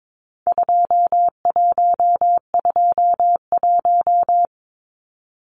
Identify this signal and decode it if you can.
Morse code, heard as a keyed tone, about 22 wpm, 700 Hz. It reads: 2121
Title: Numbers - 4-Digit 22wpm